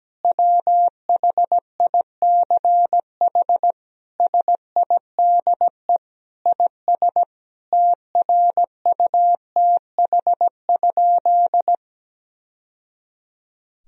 Morse code, heard as a keyed tone, about 17 wpm, 700 Hz. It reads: WHICH SIDE IS TRUTH?
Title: Sentences from Top 1000 Words - Review Part 1 - 17wpm